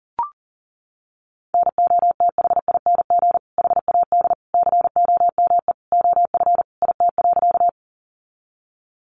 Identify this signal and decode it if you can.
Morse code, heard as a keyed tone, about 40 wpm, 700 Hz. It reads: NOTHING HAD COME OF IT.
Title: Sentences from Top 500 Words 40wpm